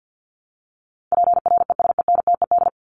Morse code, heard as a keyed tone, about 50 wpm, 700 Hz. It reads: PRESENTED
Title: ED Suffix - Top 100 Words with Base Words 50wpm